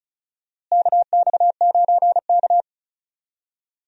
Morse code, heard as a keyed tone, about 36 wpm, 700 Hz. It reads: KX9K
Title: USA Call Signs - 4 Characters - Repeated 2x - 35wpm